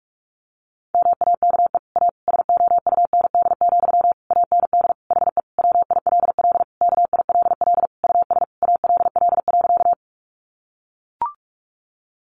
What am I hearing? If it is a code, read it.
Morse code, heard as a keyed tone, about 44 wpm, 700 Hz. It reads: MAKE A SOUND, AND HE WILL KILL US ALL.
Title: Sentences from Top 1400 Words 45wpm